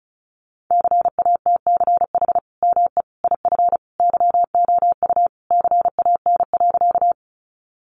Morse code, heard as a keyed tone, about 34 wpm, 700 Hz. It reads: CATCH ME IF YOU CAN.
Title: Sentences from Top 1500 Words 35wpm